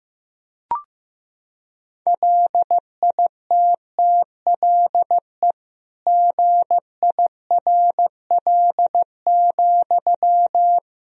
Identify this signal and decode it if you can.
Morse code, heard as a keyed tone, about 15 wpm, 700 Hz. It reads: LITTLE GIRL,
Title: Sentences from Top 400 Words and Abbreviations 15wpm